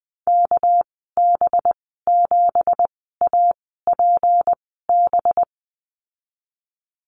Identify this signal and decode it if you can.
Morse code, heard as a keyed tone, about 20 wpm, 700 Hz. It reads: K B 7 A P B
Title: 238 Lesson 38g - Letter q - Call Signs 20wpm